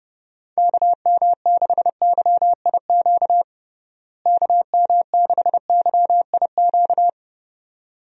Morse code, heard as a keyed tone, about 32 wpm, 700 Hz. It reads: KM6YIQ KM6YIQ
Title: USA Call Signs - 6 Characters - Repeated 2x - 30wpm